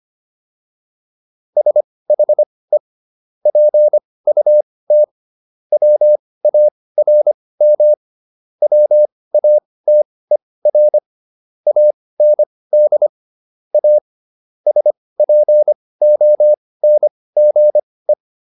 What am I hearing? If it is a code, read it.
Morse code, heard as a keyed tone, about 25 wpm, spaced as for 18 wpm, 600 Hz. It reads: SHE PUT WARM WATER AND A SPONGE